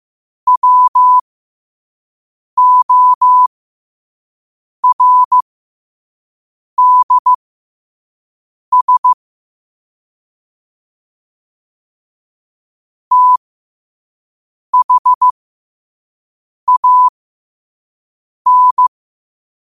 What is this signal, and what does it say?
Morse code, heard as a keyed tone, about 15 wpm, spaced as for 5 wpm, 1000 Hz. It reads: WORDS THAN
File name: qotdfiles / 5wpm